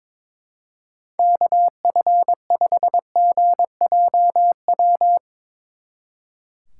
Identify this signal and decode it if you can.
Morse code, heard as a keyed tone, about 24 wpm, 700 Hz. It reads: KF5GJW
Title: Speed Racing - USA 2x3 Call Signs 22wpm